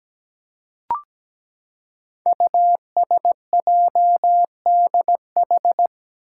Morse code, heard as a keyed tone, about 17 wpm, 700 Hz. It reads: USJDH
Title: Code Groups - 5 Letters - 17wpm